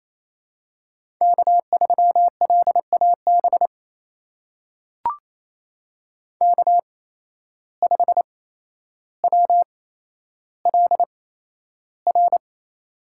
Morse code, heard as a keyed tone, about 28 wpm, spaced as for 12 wpm, 700 Hz. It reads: K3LAB K5WLR
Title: USA Call Signs - 5 Characters - 8x Character Spacing - 28wpm